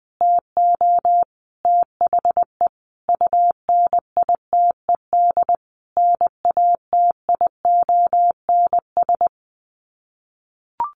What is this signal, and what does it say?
Morse code, heard as a keyed tone, about 20 wpm, 700 Hz. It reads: TO THE UNITED NATIONS
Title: Sets of 5 Words Encore2 20wpm